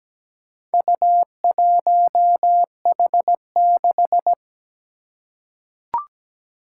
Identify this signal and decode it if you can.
Morse code, heard as a keyed tone, about 17 wpm, 700 Hz. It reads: U1H6
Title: Four Challenging Letter-Number Combinations 17wpm